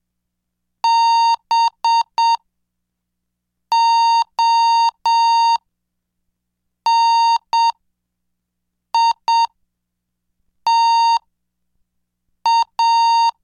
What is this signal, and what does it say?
Morse code, heard as a keyed tone, about 7 wpm, spaced as for 4 wpm, 920 Hz. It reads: BONITA